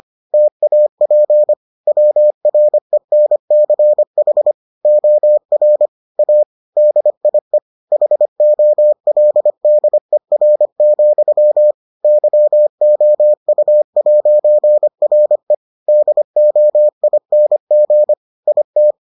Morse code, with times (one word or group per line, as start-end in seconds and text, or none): 0.3-1.5 TAP
1.9-4.5 WRENCH
4.8-5.9 OR
6.2-6.4 A
6.8-7.6 DIE
7.9-11.7 HOLDER,
12.0-15.5 YOU'RE
15.9-18.1 DOING
18.5-18.9 IT